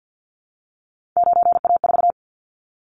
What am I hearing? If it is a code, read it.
Morse code, heard as a keyed tone, about 50 wpm, 700 Hz. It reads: 9A4
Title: Callsign Prefixes for Top 25 DXCC - Repeated 3x - 50wpm